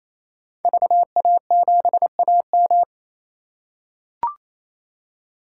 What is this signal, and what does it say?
Morse code, heard as a keyed tone, about 28 wpm, 700 Hz. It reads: VA7AM
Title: K1USN SST QSO Practice - 28wpm